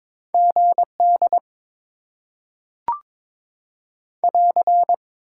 Morse code, heard as a keyed tone, about 22 wpm, 700 Hz. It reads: GD <AR>
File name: Top 90 QSO Elements - 6x Character Spacing - 22wpm